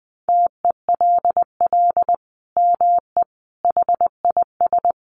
Morse code, heard as a keyed tone, about 19 wpm, 700 Hz. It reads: TELL ME HIS
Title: Sentences from Top 200 Words 20wpm